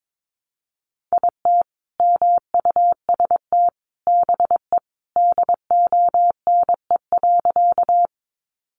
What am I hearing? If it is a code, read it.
Morse code, heard as a keyed tone, about 22 wpm, 700 Hz. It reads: IT MUST BE DONE.
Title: Sentences from Top 500 Words 22wpm